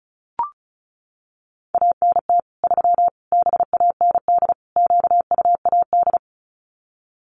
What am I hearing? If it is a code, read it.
Morse code, heard as a keyed tone, about 34 wpm, 700 Hz. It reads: ANT 3 BAND QUAD
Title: Common QSO Phrases 35wpm